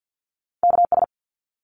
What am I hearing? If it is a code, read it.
Morse code, heard as a keyed tone, about 50 wpm, 700 Hz. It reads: KS
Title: ARRL Sections 50wpm